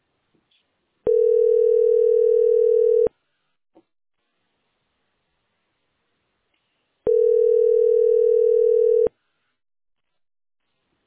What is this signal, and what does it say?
Ringback tone (North America), 440 + 480 Hz.